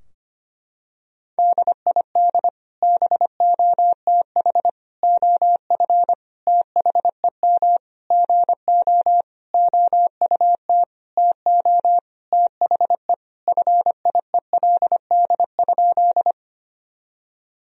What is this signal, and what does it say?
Morse code, heard as a keyed tone, about 25 wpm, 700 Hz. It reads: DID BOTH OF THEM GO OUT TO THE FIELD?
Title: Sentences from Top 1000 Words - Review Part 2 - 25wpm